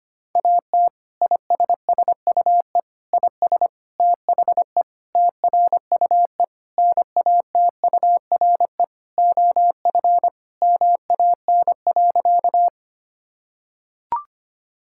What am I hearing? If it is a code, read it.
Morse code, heard as a keyed tone, about 25 wpm, 700 Hz. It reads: AT ISSUE IS THE TRUE NATURE OF MAN.
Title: Sentences from Top 1000 Words - Review Part 6 - 25wpm